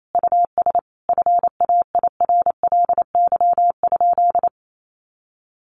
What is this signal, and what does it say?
Morse code, heard as a keyed tone, about 28 wpm, 700 Hz. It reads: US FAIRLY?
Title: Sentences from Top 1100 Words 28wpm